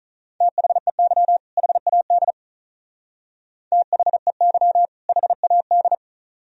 Morse code, heard as a keyed tone, about 42 wpm, 700 Hz. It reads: THEY HAD THEY HAD
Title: Speed-Racing - Sets of 2 Words 35wpm